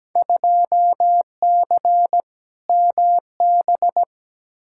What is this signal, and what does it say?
Morse code, heard as a keyed tone, about 17 wpm, 700 Hz. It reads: 2C MB
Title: Field Day Exchanges Encore - 17wpm